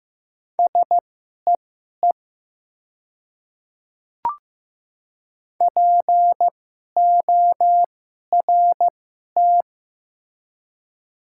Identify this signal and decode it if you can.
Morse code, heard as a keyed tone, about 15 wpm, spaced as for 11 wpm, 700 Hz. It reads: SEE PORT
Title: 2x Character Spacing - Top 300 Words - 15wpm